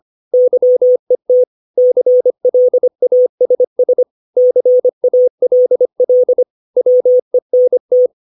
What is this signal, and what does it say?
Morse code, heard as a keyed tone, about 25 wpm, 500 Hz. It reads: YET CLASS CALL WENT
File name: 632 words / 632 Words 25WPM